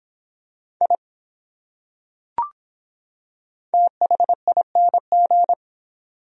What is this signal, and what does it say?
Morse code, heard as a keyed tone, about 23 wpm, 700 Hz. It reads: I THING